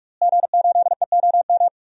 Morse code, heard as a keyed tone, about 43 wpm, 700 Hz. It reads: G8EOM
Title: DXCC-100 Call Signs - 5 Characters - Repeated 2x - 45wpm